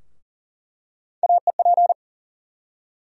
Morse code, heard as a keyed tone, about 40 wpm, 700 Hz. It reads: AEP
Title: USA Call Sign Postfixes - 3 Letters 40wpm